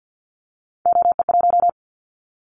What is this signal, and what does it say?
Morse code, heard as a keyed tone, about 49 wpm, 700 Hz. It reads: OE1
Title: Callsign Prefixes for Top 100 DXCC - 50wpm